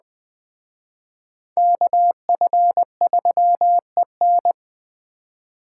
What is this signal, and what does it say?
Morse code, heard as a keyed tone, about 20 wpm, 700 Hz. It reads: KF3EN
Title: Speed Racing - USA 2x2 Call Signs 17wpm